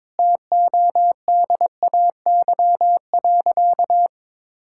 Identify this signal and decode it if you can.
Morse code, heard as a keyed tone, about 22 wpm, 700 Hz. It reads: TODAY.
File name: Sentences with 9-Word Phrases - 22wpm